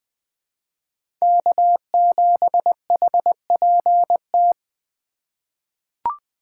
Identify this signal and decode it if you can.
Morse code, heard as a keyed tone, about 20 wpm, 700 Hz. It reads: K7HPT
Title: K1USN SST QSO Practice - 20wpm